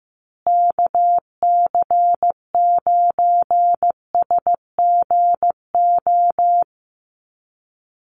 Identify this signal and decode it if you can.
Morse code, heard as a keyed tone, about 15 wpm, 700 Hz. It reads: KC9SGO
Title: USA Call Signs Repeated 3x 15wpm